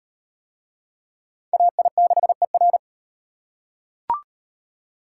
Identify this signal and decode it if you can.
Morse code, heard as a keyed tone, about 38 wpm, 700 Hz. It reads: AI6ER